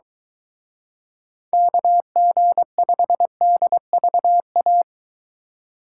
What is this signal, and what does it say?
Morse code, heard as a keyed tone, about 23 wpm, 700 Hz. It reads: KG5DVA